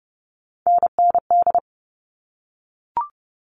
Morse code, heard as a keyed tone, about 30 wpm, 700 Hz. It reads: NND
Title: All Cut Numbers - 3-Digit 30wpm